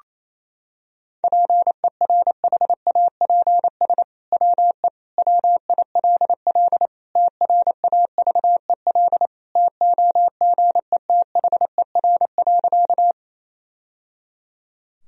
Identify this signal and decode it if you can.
Morse code, heard as a keyed tone, about 27 wpm, 700 Hz. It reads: PERHAPS WE WILL TRAVEL TOGETHER.
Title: Sentences from Top 1000 Words - Review Part 2 - 28wpm